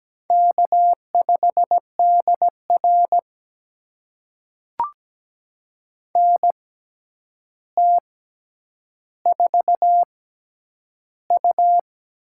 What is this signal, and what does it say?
Morse code, heard as a keyed tone, about 17 wpm, spaced as for 9 wpm, 700 Hz. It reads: K5DR NT4U
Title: USA Call Signs - 4 Characters - 6x Character Spacing - 17wpm